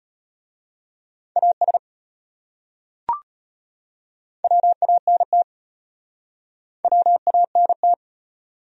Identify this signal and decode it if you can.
Morse code, heard as a keyed tone, about 35 wpm, 700 Hz. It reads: AS WANT WANT